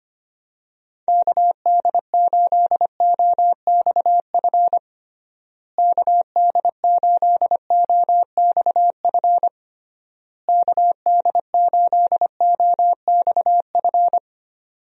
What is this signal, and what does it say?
Morse code, heard as a keyed tone, about 25 wpm, 700 Hz. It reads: KD8OXF KD8OXF KD8OXF
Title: USA Call Signs - 6 Characters - Repeated 3x - 25wpm